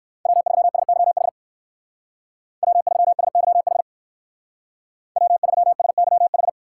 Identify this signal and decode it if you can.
Morse code, heard as a keyed tone, about 55 wpm, 700 Hz. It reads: W3SYH W3SYH W3SYH
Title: Speed Racing - USA 1x3 Call Signs 45wpm